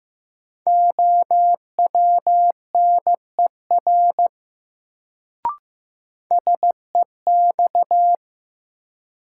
Morse code, heard as a keyed tone, about 15 wpm, 700 Hz. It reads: OWNER SEX
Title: Sentences from Top 1600 Words - New Word Drill - 15wpm